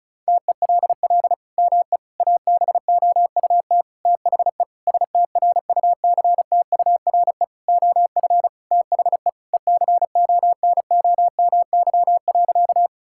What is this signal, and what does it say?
Morse code, heard as a keyed tone, about 34 wpm, 700 Hz. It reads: TELL ME ABOUT THE STRUCTURE OF THE ECONOMY.
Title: Sentences from Top 1000 Words - Review Part 6 - 35wpm